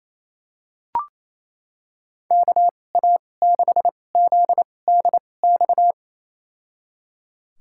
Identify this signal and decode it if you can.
Morse code, heard as a keyed tone, about 28 wpm, spaced as for 20 wpm, 690 Hz. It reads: KA6ZDX
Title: USA Call Signs - 2x Character Spacing - 28wpm